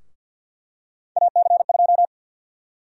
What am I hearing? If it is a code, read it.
Morse code, heard as a keyed tone, about 51 wpm, 700 Hz. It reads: AC2
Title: USA Call Sign Prefixes - Repeated 2x - 50wpm